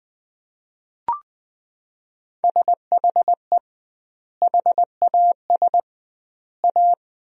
Morse code, heard as a keyed tone, about 20 wpm, 700 Hz. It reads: SHE HAS A
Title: 2x Word-Spacing - Sentences from Top 200 Words - 20wpm